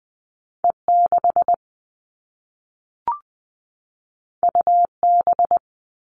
Morse code, heard as a keyed tone, about 20 wpm, 700 Hz. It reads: E6 UB
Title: All Cut Numbers - 2-Digit 20wpm